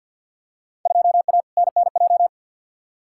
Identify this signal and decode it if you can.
Morse code, heard as a keyed tone, about 53 wpm, 700 Hz. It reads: 2A NNJ
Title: Field Day Exchanges - 50wpm